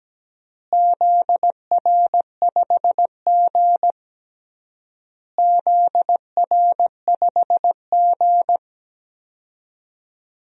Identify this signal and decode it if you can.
Morse code, heard as a keyed tone, about 17 wpm, 700 Hz. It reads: ZR5G ZR5G
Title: DXCC-100 Call Signs - 4 Characters - Repeated 3x - 17wpm